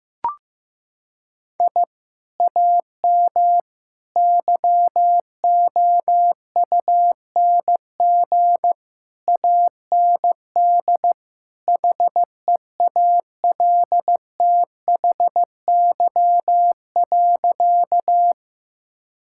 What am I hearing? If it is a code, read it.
Morse code, heard as a keyed tone, about 15 wpm, 700 Hz. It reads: I AM YOUNG AND HEALTHY.
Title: Sentences from Top 1000 Words - Review Part 5 - 15wpm